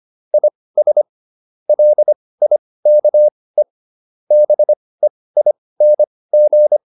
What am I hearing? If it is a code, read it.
Morse code, heard as a keyed tone, about 25 wpm, spaced as for 18 wpm, 600 Hz. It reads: IS LIKE BEING